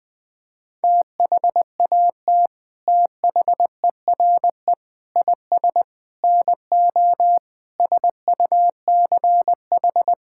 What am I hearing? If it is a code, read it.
Morse code, heard as a keyed tone, about 20 wpm, 700 Hz. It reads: THAT THERE IS NO SUCH
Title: Sets of 8 Words - 20wpm